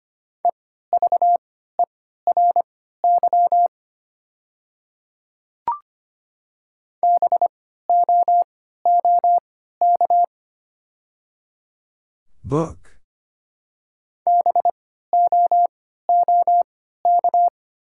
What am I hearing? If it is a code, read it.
Morse code, heard as a keyed tone, about 25 wpm, spaced as for 14 wpm, 700 Hz. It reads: EVERY BOOK BOOK
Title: 3x Character Spacing - Top 300 Words - 25wpm